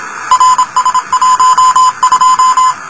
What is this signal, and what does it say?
Morse code, heard as a keyed tone, about 27 wpm, 1020 Hz, with noise. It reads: RS12
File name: luxorion / Radio / rs12-23mar02-telemetry-cw